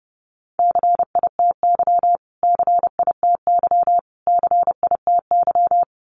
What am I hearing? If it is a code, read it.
Morse code, heard as a keyed tone, about 30 wpm, 700 Hz. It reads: CITY CITY CITY
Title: Top 500 Words Repeated 30wpm